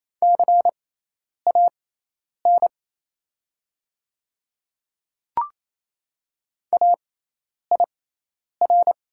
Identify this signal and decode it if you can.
Morse code, heard as a keyed tone, about 27 wpm, spaced as for 10 wpm, 700 Hz. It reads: CAN AIR
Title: 6x Character Spacing - Top 200 Words - 28wpm